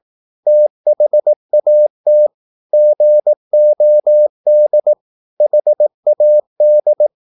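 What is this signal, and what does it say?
Morse code, heard as a keyed tone, about 18 wpm, 600 Hz. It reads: THAT GOD HAD
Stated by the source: Morse practice at 18 wpm, English text, from CW med Gnister.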